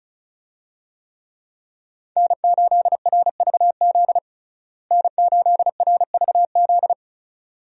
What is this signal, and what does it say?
Morse code, heard as a keyed tone, about 34 wpm, 700 Hz. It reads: N8RVZ N8RVZ
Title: USA Call Signs - 5 Characters - Repeated 2x - 35wpm